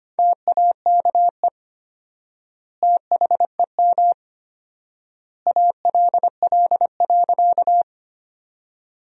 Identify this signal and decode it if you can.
Morse code, heard as a keyed tone, about 25 wpm, 700 Hz. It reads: TAKE THEM ALL.
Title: 4x Word-Spacing - Sentences from Top 200 Words - 25wpm